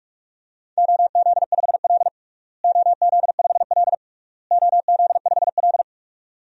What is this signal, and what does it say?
Morse code, heard as a keyed tone, about 46 wpm, 700 Hz. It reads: OZ5L OZ5L OZ5L
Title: DXCC-100 Call Signs - 4 Characters - Repeated 3x - 45wpm